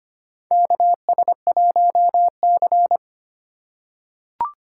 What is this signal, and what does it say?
Morse code, heard as a keyed tone, about 25 wpm, 700 Hz. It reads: KS1C
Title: USA Call Signs - 4 Characters - 3x Character Spacing - 25wpm